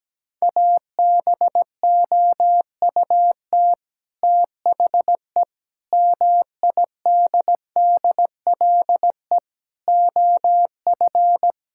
Morse code, heard as a keyed tone, about 17 wpm, 700 Hz. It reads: ABOUT THE MIDDLE OF
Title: Sentences from Top 1000 Words - Review Part 6 - 17wpm